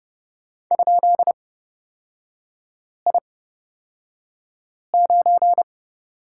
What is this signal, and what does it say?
Morse code, heard as a keyed tone, about 30 wpm, spaced as for 5 wpm, 700 Hz. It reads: ?I9
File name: Lesson 36e - BT - Three Characters 30wpm